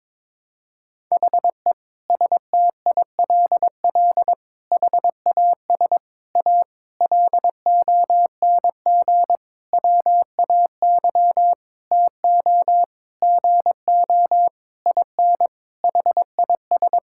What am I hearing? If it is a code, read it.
Morse code, heard as a keyed tone, about 21 wpm, 700 Hz. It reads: HE STILL HAS A LONG WAY TO GO IN HIS
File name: Sentences with 7-Word Phrases - 22wpm